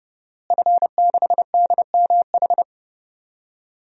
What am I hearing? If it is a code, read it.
Morse code, heard as a keyed tone, about 30 wpm, 700 Hz. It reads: F6DMH
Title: DXCC-100 Call Signs - 5 Characters - Repeated 3x - 30wpm